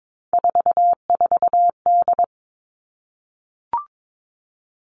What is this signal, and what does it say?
Morse code, heard as a keyed tone, about 22 wpm, 700 Hz. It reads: 44D
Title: All Cut Numbers - 3-Digit 22wpm